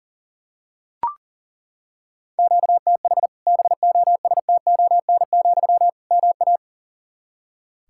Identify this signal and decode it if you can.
Morse code, heard as a keyed tone, about 40 wpm, 700 Hz. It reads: QTH BOSTON, MA
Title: Practice QSO - Exchange 1 - Beginning - 40wpm